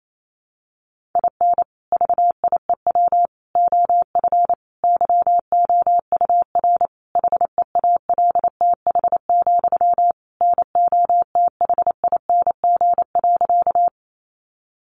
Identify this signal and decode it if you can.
Morse code, heard as a keyed tone, about 28 wpm, 700 Hz. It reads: IN VIEW OF YOUR HEALTH, NOTHING.
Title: Sentences from Top 900 Words 28wpm